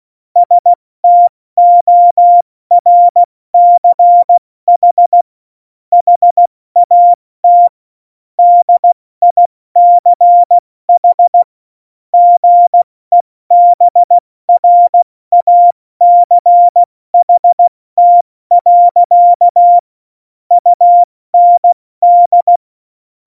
Morse code, heard as a keyed tone, about 16 wpm, 700 Hz. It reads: STORCH HAT DICH GEBRACHT. UND
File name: Morsekurs / Witze / Witz 19. Februar 2019, 70 F80